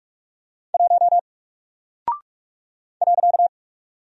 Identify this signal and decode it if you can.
Morse code, heard as a keyed tone, about 44 wpm, 700 Hz. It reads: 1 .
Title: Single Character (Letter:Number:Punctuation:Prosign) 45wpm